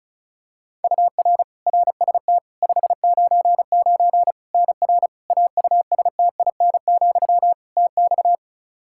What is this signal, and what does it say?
Morse code, heard as a keyed tone, about 36 wpm, 700 Hz. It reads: UR RST 599 NR AUSTIN, TX
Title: Practice QSO - Exchange 1 - Beginning - 35wpm